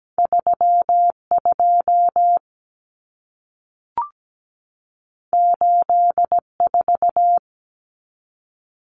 Morse code, heard as a keyed tone, about 17 wpm, 700 Hz. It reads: 32 84
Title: Numbers - 2-Digit 17wpm